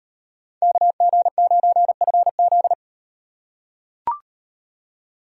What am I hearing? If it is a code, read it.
Morse code, heard as a keyed tone, about 38 wpm, 700 Hz. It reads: KG9FZ